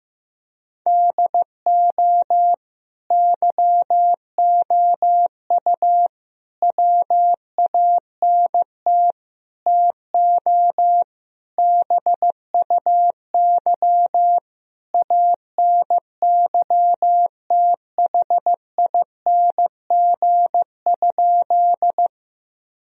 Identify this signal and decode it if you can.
Morse code, heard as a keyed tone, about 16 wpm, 700 Hz. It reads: DO YOU WANT TO BUY ANYTHING?
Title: Sentences from Top 1000 Words - Review Part 4 - 15wpm